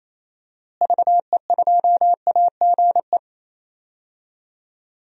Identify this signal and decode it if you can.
Morse code, heard as a keyed tone, about 28 wpm, 700 Hz. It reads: VE2AGE
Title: DXCC-100 Call Signs - 6 Characters - Repeated 3x - 28wpm